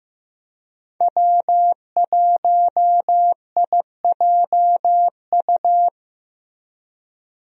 Morse code, heard as a keyed tone, about 15 wpm, 700 Hz. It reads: W1IJU
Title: USA Call Signs - 5 Characters - Repeated 3x - 15wpm